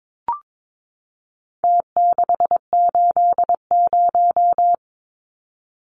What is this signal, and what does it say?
Morse code, heard as a keyed tone, about 22 wpm, 700 Hz. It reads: T680
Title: Numbers - Contest Style - 4-Digit 22wpm